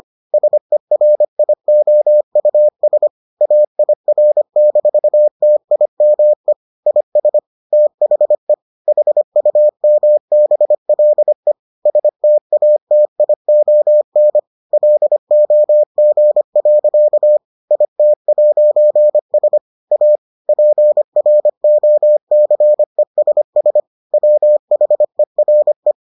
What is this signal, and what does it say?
Morse code, heard as a keyed tone, about 25 wpm, 600 Hz. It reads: SERIOUS AIR-TIME IS THE HUMBLE STATION LOG. IT'S A PROCESS WHERE